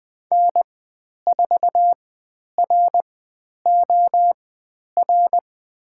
Morse code, heard as a keyed tone, about 20 wpm, spaced as for 10 wpm, 700 Hz. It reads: N4ROR
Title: 102 Lesson 17g - Letter u - Call Signs 20wpm